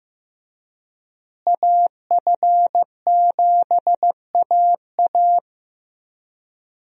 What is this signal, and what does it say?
Morse code, heard as a keyed tone, about 15 wpm, 700 Hz. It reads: AF7AA
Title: USA Call Signs - 5 Characters - Repeated 3x - 15wpm